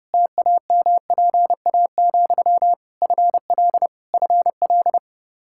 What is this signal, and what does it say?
Morse code, heard as a keyed tone, about 30 wpm, 700 Hz. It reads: TAMPA, FL FL
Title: K1USN SST QSO Practice 30wpm